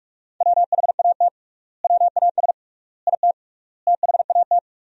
Morse code, heard as a keyed tone, about 48 wpm, 700 Hz. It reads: WHAT WAS IT THAT
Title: 3x Word-Spacing - Sentences from Top 300 Words - 45wpm